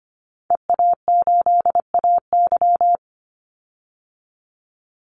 Morse code, heard as a keyed tone, about 25 wpm, 700 Hz. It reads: EA8AY
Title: Call Signs - 25wpm